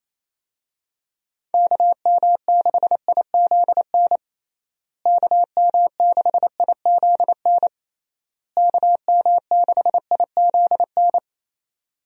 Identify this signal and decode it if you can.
Morse code, heard as a keyed tone, about 28 wpm, 700 Hz. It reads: KM6IZN KM6IZN KM6IZN
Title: USA Call Signs - 6 Characters - Repeated 3x - 28wpm